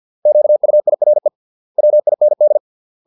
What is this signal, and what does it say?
Morse code, heard as a keyed tone, about 50 wpm, 600 Hz. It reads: QUIRE WIND